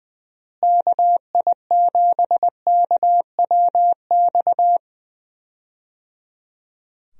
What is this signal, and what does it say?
Morse code, heard as a keyed tone, about 20 wpm, 700 Hz. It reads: KI7KWX
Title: USA Call Signs - 6 Characters - Repeated 3x - 20wpm